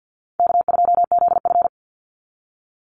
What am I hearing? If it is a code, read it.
Morse code, heard as a keyed tone, about 50 wpm, 700 Hz. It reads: K2ZF
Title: Call Signs - 50wpm